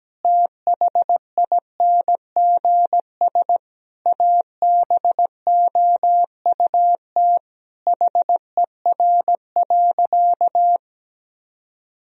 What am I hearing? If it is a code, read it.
Morse code, heard as a keyed tone, about 17 wpm, 700 Hz. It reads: THINGS ABOUT HER.
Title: Sentences from Top 1000 Words - Review Part 4 - 17wpm